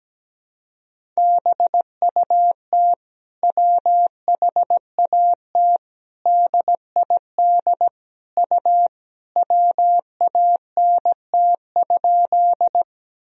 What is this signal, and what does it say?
Morse code, heard as a keyed tone, about 17 wpm, 700 Hz. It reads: BUT WHAT DID U WANT?
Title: Sentences from Top 200 Words and Abbreviations 17wpm